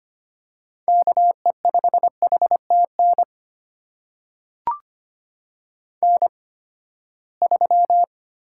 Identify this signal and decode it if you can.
Morse code, heard as a keyed tone, about 25 wpm, spaced as for 17 wpm, 700 Hz. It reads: KE5HTN N3
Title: USA Call Signs - 8x Character Spacing - 25wpm